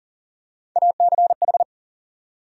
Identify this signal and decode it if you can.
Morse code, heard as a keyed tone, about 40 wpm, 700 Hz. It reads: ACH
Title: USA Call Sign Postfixes - 3 Letters 40wpm